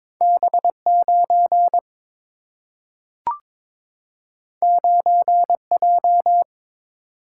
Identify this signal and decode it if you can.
Morse code, heard as a keyed tone, about 22 wpm, 700 Hz. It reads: B9 9J
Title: Single-Letter Single-Number 22wpm